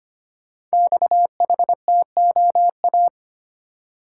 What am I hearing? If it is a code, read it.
Morse code, heard as a keyed tone, about 25 wpm, 700 Hz. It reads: XHTOA
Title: Code Groups - 5 Letters - 25wpm